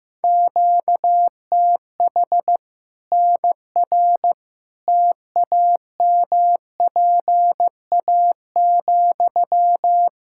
Morse code, heard as a keyed tone, about 15 wpm, 700 Hz. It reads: QTH NR TAMPA,
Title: K1USN SST QSO Practice 15wpm